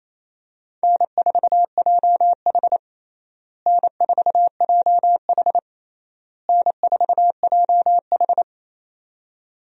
Morse code, heard as a keyed tone, about 28 wpm, 700 Hz. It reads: N4JH N4JH N4JH
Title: USA Call Signs - 4 Characters - Repeated 3x - 28wpm